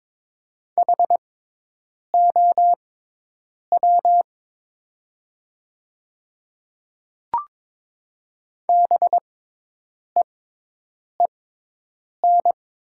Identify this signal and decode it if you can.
Morse code, heard as a keyed tone, about 22 wpm, spaced as for 8 wpm, 700 Hz. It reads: HOW BEEN
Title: 6x Character Spacing - Top 100 Words - 22wpm